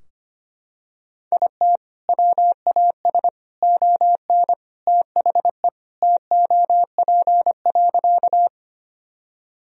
Morse code, heard as a keyed tone, about 25 wpm, 700 Hz. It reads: IT WAS ON THE TOP.
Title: Sentences from Top 1000 Words - Review Part 2 - 25wpm